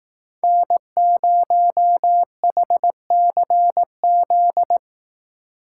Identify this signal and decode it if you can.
Morse code, heard as a keyed tone, about 18 wpm, 700 Hz. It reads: N0HCZ
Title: Speed Racing - USA 1x3 Call Signs 15wpm